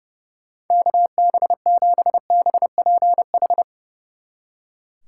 Morse code, heard as a keyed tone, about 35 wpm, 700 Hz. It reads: KB7BPH
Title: Speed Racing - USA 2x3 Call Signs 30wpm